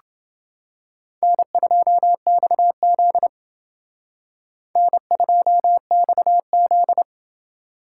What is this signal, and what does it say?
Morse code, heard as a keyed tone, about 29 wpm, 700 Hz. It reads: N2XZ N2XZ